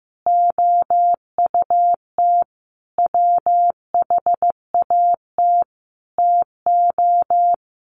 Morse code, heard as a keyed tone, about 15 wpm, 700 Hz. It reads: OUT WHAT TO
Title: Sets of 5 Words Encore2 15wpm